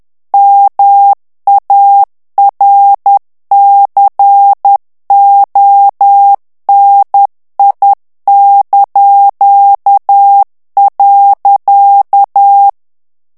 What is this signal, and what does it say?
Morse code, heard as a keyed tone, about 10 wpm, 800 Hz. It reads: MARCONI).
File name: morse mp3 / ITALIEN texte clair 1 vitesse 10 mots minute